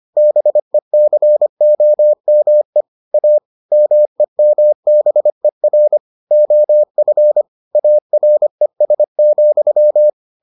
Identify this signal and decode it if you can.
Morse code, heard as a keyed tone, about 25 wpm, 600 Hz. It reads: BECOME A MEMBER OF ARES,